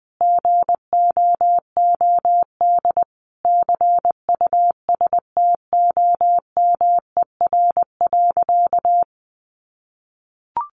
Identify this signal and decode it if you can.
Morse code, heard as a keyed tone, about 20 wpm, 700 Hz. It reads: GOOD CUSTOMER.
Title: Sentences from Top 1100 Words 20wpm